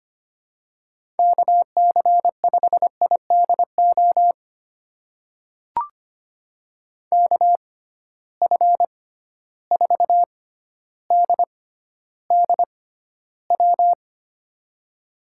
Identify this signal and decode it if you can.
Morse code, heard as a keyed tone, about 25 wpm, spaced as for 13 wpm, 700 Hz. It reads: KC5IDO KF4DDW
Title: USA Call Signs - 6 Characters - 6x Character Spacing - 25wpm